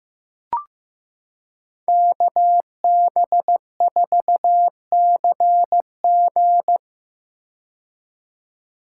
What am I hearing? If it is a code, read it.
Morse code, heard as a keyed tone, about 15 wpm, 700 Hz. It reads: KB4CG
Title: K1USN SST QSO Practice - 15wpm